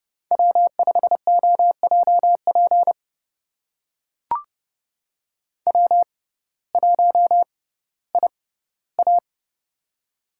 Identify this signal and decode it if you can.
Morse code, heard as a keyed tone, about 30 wpm, spaced as for 17 wpm, 700 Hz. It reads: W5OJP W1IA
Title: USA Call Signs - 6x Character Spacing - 30wpm